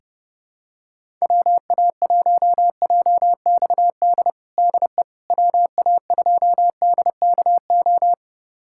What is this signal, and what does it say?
Morse code, heard as a keyed tone, about 30 wpm, 700 Hz. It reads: WA1JXD DE WA2DKO
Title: USA Callsign DE Callsign - 6 Characters - 30wpm